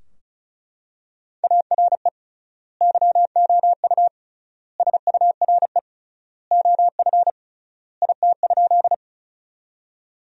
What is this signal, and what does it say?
Morse code, heard as a keyed tone, about 36 wpm, 700 Hz. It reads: ARE YOU SURE OF IT?
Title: 3x Word-Spacing - Sentences from Top 300 Words - 35wpm